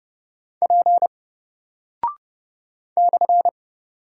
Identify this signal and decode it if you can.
Morse code, heard as a keyed tone, about 30 wpm, 700 Hz. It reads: P /
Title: Single Character (Letter:Number:Punctuation:Prosign) Alt 30wpm